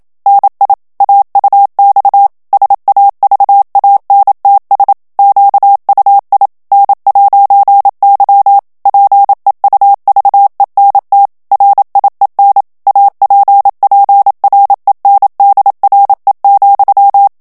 Morse code, heard as a keyed tone, about 27 wpm, 800 Hz. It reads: NI AUX SAVANTS QUI N'Y PEUVENT RIEN APPRENDRE,
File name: FRANCAIS TEXTE CLAIR 2 VITESSE 26 MOTS MINUTES